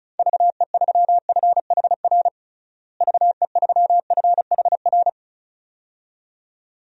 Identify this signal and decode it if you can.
Morse code, heard as a keyed tone, about 36 wpm, 700 Hz. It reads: VE3FHR VE3FHR
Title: DXCC-100 Call Signs - 6 Characters - Repeated 2x - 35wpm